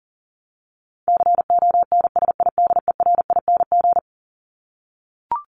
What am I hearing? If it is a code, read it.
Morse code, heard as a keyed tone, about 40 wpm, 700 Hz. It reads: CONSIDERING
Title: ING Suffix - Top 100 Words with Base Words 40wpm